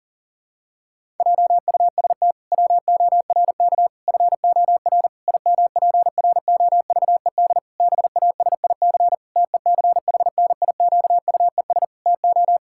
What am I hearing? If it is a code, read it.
Morse code, heard as a keyed tone, about 40 wpm, 700 Hz. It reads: JUST WORK FOR IMPROVED BASIC TECHNIQUES TO
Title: Pangram Challenge v2 - 40wpm